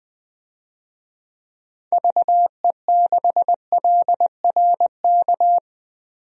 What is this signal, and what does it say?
Morse code, heard as a keyed tone, about 21 wpm, 700 Hz. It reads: VE6LRK
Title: DXCC-100 Call Signs - 6 Characters - 20wpm